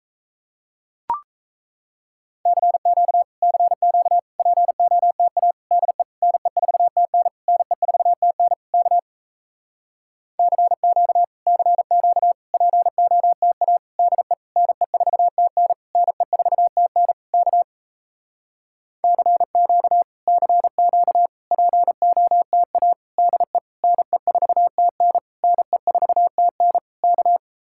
Morse code, heard as a keyed tone, about 37 wpm, 700 Hz. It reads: CQ CQ POTA DE NE4TN NE4TN K CQ CQ POTA DE NE4TN NE4TN K CQ CQ POTA DE NE4TN NE4TN K